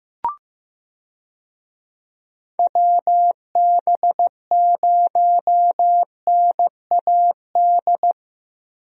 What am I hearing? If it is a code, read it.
Morse code, heard as a keyed tone, about 15 wpm, 700 Hz. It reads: WB0NAD
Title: USA Call Signs - 6 Characters - 15wpm